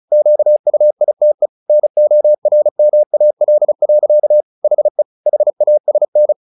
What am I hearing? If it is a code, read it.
Morse code, heard as a keyed tone, about 34 wpm, 600 Hz. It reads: QUITE NORMAL. HE HASN
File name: the old man and the sea / the old man and the sea 1 35wpm